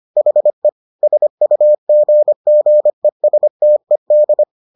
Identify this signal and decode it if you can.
Morse code, heard as a keyed tone, about 25 wpm, 600 Hz. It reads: HE SUGGESTED